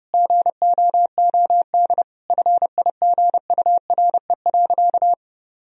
Morse code, heard as a keyed tone, about 30 wpm, 700 Hz. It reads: GOOD FIGURE.
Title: Sentences from Top 1000 Words - Review Part 2 - 30wpm